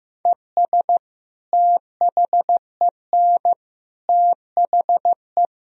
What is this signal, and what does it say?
Morse code, heard as a keyed tone, about 15 wpm, 700 Hz. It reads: ES THEN THE
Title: Sentences from Top 500 Words and Abbreviations 15wpm